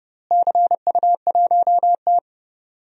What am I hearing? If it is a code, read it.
Morse code, heard as a keyed tone, about 30 wpm, 700 Hz. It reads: CU1T
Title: DXCC-100 Call Signs - 4 Characters - Repeated 3x - 30wpm